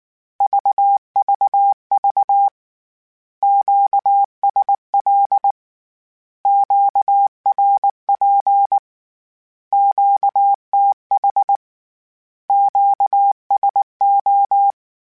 Morse code, heard as a keyed tone, about 19 wpm, 800 Hz. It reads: VVV QSL QRP QTH QSO
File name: Q Calls / Q Calls-mp3s / Q Calls Once @18